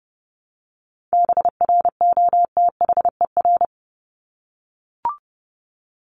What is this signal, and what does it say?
Morse code, heard as a keyed tone, about 30 wpm, 700 Hz. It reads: BROTHER
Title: Top 1751-2000 Words 30wpm